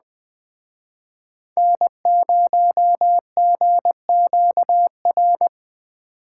Morse code, heard as a keyed tone, about 20 wpm, 700 Hz. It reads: N0GQR
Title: Speed Racing - USA 1x3 Call Signs 17wpm